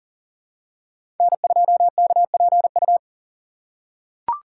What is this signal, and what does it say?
Morse code, heard as a keyed tone, about 40 wpm, 700 Hz. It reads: N2KPU
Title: USA Call Signs - 6x Character Spacing - 40wpm